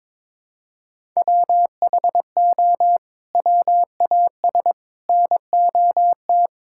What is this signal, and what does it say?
Morse code, heard as a keyed tone, about 22 wpm, 700 Hz. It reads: WHO WAS NOT
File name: practice / 22wpm / Sets of 7 Words - 22wpm